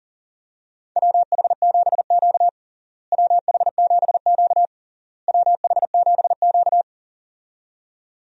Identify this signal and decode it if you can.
Morse code, heard as a keyed tone, about 40 wpm, 700 Hz. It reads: WH7Q WH7Q WH7Q
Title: USA Call Signs - 4 Characters - Repeated 3x - 40wpm